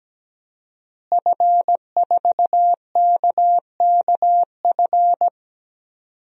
Morse code, heard as a keyed tone, about 17 wpm, 700 Hz. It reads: F4KKF
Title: DXCC-100 Call Signs - 5 Characters - Repeated 2x - 17wpm